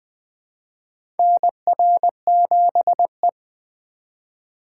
Morse code, heard as a keyed tone, about 20 wpm, 700 Hz. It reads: NR7E
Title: USA Call Signs - 4 Characters - 3x Character Spacing - 20wpm